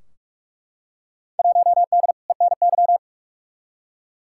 Morse code, heard as a keyed tone, about 48 wpm, 700 Hz. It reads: 1D ENY
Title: Field Day Exchanges Encore - 45wpm